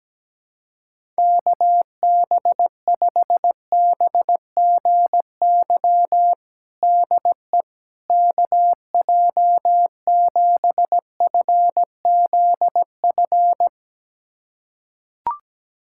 Morse code, heard as a keyed tone, about 17 wpm, 700 Hz. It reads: KB5BGY DE KJ7FZF
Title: USA Callsign DE Callsign 17wpm